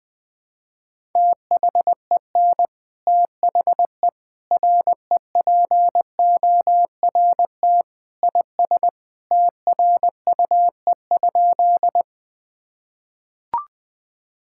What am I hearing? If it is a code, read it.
Morse code, heard as a keyed tone, about 20 wpm, 700 Hz. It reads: THEN THE REPORT IS TRUE?
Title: Sentences from Top 1000 Words - Review Part 6 - 20wpm